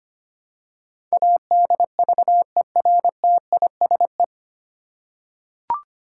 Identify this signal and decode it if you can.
Morse code, heard as a keyed tone, about 25 wpm, 700 Hz. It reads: ADVERTISE
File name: practice / 25wpm / 9 Letter Words - First 3 Letters Given - 25wpm